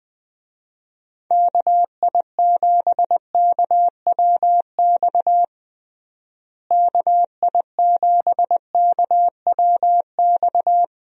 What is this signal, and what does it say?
Morse code, heard as a keyed tone, about 20 wpm, 700 Hz. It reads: KI7KWX KI7KWX
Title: USA Call Signs - 6 Characters - Repeated 3x - 20wpm